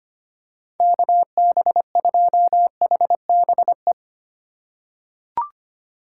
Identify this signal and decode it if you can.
Morse code, heard as a keyed tone, about 25 wpm, 700 Hz. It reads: KB2HBE
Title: USA Call Signs - 6 Characters - 3x Character Spacing - 25wpm